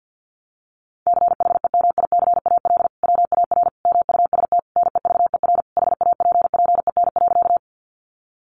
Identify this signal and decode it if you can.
Morse code, heard as a keyed tone, about 50 wpm, 700 Hz. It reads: CHEMICAL WAR MUST NEVER HAPPEN.
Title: Sentences from Top 1200 Words 50wpm